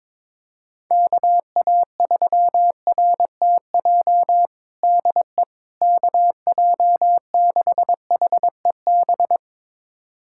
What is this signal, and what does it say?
Morse code, heard as a keyed tone, about 22 wpm, 700 Hz. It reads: KA3RTJ DE KJ6HEB
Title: USA Callsign DE Callsign 22wpm